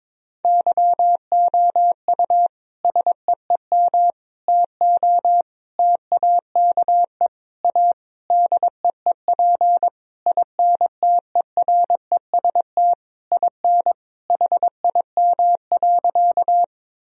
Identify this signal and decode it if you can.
Morse code, heard as a keyed tone, about 22 wpm, 700 Hz. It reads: YOU SEEM TO TAKE A DEEP INTEREST IN HIM.
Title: Sentences from Top 1000 Words - Review Part 3 - 22wpm